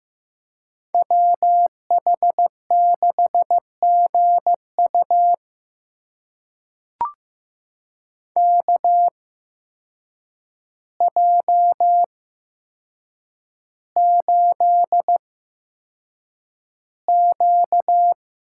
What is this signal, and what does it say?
Morse code, heard as a keyed tone, about 15 wpm, spaced as for 7 wpm, 700 Hz. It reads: WH6GU KJ8Q
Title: USA Call Signs - 8x Character Spacing - 15wpm